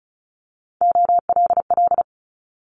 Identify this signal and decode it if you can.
Morse code, heard as a keyed tone, about 34 wpm, 700 Hz. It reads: OLL
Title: Multi-letter Phonemes Quick Fire 35wpm